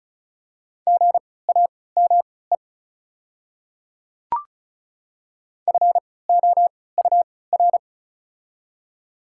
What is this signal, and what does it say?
Morse code, heard as a keyed tone, about 36 wpm, spaced as for 20 wpm, 700 Hz. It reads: GAME FOUR
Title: 3x Character Spacing - Top 500 Words - 35wpm